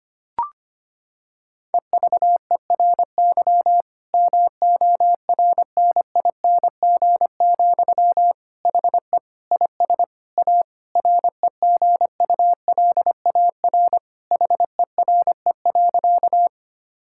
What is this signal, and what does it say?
Morse code, heard as a keyed tone, about 25 wpm, 700 Hz. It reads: EVERY MORNING, HE IS A REGULAR HERE.
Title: Sentences from Top 1000 Words - Review Part 6 - 25wpm